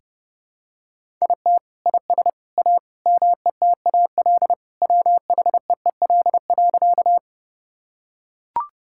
Morse code, heard as a keyed tone, about 32 wpm, 700 Hz. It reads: IT IS A METAL WHEEL.
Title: Sentences from Top 1000 Words - Review Part 6 - 30wpm